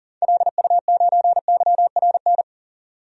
Morse code, heard as a keyed tone, about 40 wpm, 700 Hz. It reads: LU9YRN
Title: DXCC-100 Call Signs - 6 Characters - 40wpm